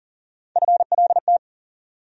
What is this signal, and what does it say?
Morse code, heard as a keyed tone, about 40 wpm, 700 Hz. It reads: FLT
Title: USA Call Sign Postfixes - 3 Letters 40wpm